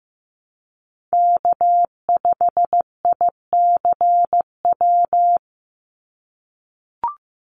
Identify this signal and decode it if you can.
Morse code, heard as a keyed tone, about 15 wpm, 700 Hz. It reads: K5ICW
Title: USA Call Signs 15wpm